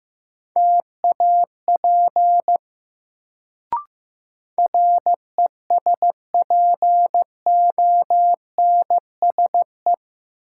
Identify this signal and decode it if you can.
Morse code, heard as a keyed tone, about 15 wpm, 700 Hz. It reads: TAP RESPONSE
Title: Sentences from Top 1400 Words - New Word Drill - 15wpm